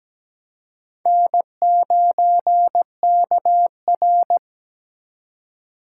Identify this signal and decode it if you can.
Morse code, heard as a keyed tone, about 17 wpm, 700 Hz. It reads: N9KR
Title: K1USN SST QSO Practice - 17wpm